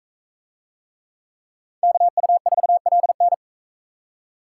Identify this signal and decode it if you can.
Morse code, heard as a keyed tone, about 44 wpm, 700 Hz. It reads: KU4LN